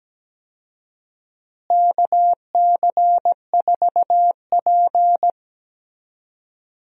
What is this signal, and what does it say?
Morse code, heard as a keyed tone, about 17 wpm, 700 Hz. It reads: KC4P
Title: USA Call Signs - 4 Characters - Repeated 2x - 17wpm